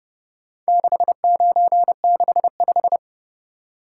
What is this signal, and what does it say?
Morse code, heard as a keyed tone, about 30 wpm, 700 Hz. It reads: 6965